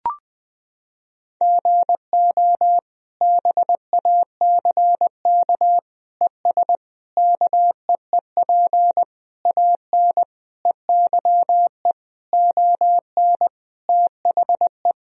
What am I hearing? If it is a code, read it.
Morse code, heard as a keyed tone, about 21 wpm, 700 Hz. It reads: GO BACK ES KEEP AN EYE ON THE
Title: Sentences from Top 300 Words and Abbreviations 20wpm